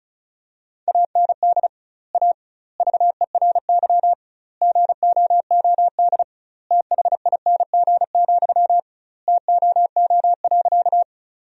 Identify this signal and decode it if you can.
Morse code, heard as a keyed tone, about 36 wpm, 700 Hz. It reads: AND A VERY GOOD THING, TOO.
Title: 2x Word-Spacing - Sentences from Top 200 Words - 35wpm